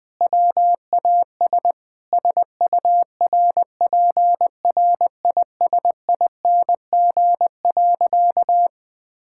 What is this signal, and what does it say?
Morse code, heard as a keyed tone, about 20 wpm, 700 Hz. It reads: WAS SURPRISING.
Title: Sentences with 4-Word Phrases - 20wpm